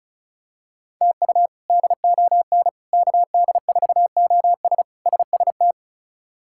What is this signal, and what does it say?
Morse code, heard as a keyed tone, about 36 wpm, 700 Hz. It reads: TU DON KD4OS SST
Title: K1USN SST QSO Practice - 35wpm